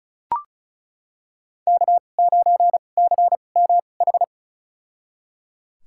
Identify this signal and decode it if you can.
Morse code, heard as a keyed tone, about 36 wpm, spaced as for 25 wpm, 700 Hz. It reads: K9CMH
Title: USA Call Signs - 5 Characters - 2x Character Spacing - 35wpm